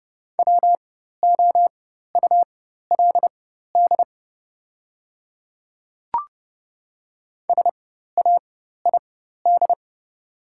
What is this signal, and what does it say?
Morse code, heard as a keyed tone, about 30 wpm, spaced as for 14 wpm, 700 Hz. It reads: WOULD SAID
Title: 4x Character Spacing - Top 100 Words - 30wpm